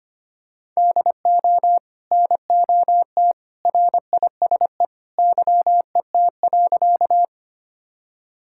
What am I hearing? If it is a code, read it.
Morse code, heard as a keyed tone, about 25 wpm, 700 Hz. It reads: DO NOT RISE YET.
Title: Sentences from Top 1000 Words - Review Part 6 - 25wpm